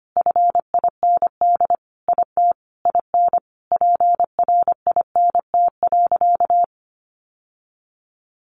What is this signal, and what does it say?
Morse code, heard as a keyed tone, about 25 wpm, 700 Hz. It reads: FIND IT IN PRINT.
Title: Sentences from Top 1400 Words 25wpm